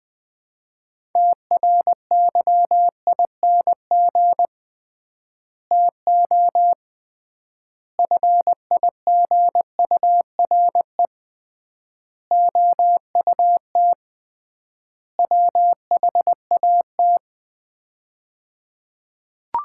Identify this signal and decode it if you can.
Morse code, heard as a keyed tone, about 20 wpm, 700 Hz. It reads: TRYING TO FIGURE OUT WHAT
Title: 3x Word-Spacing - Sets of 5 Words - 20wpm